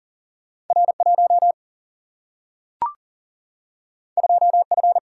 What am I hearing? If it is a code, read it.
Morse code, heard as a keyed tone, about 40 wpm, 700 Hz. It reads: R 1 2 F
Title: Single-Letter Single-Number 40wpm